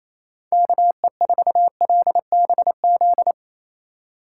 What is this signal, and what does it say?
Morse code, heard as a keyed tone, about 28 wpm, 700 Hz. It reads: KE4LBZ
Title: USA Call Signs - 6 Characters - 4x Character Spacing - 28wpm